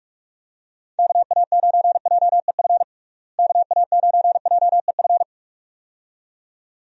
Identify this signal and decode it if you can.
Morse code, heard as a keyed tone, about 46 wpm, 700 Hz. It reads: KA9JEF KA9JEF
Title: USA Call Signs - 6 Characters - Repeated 2x - 45wpm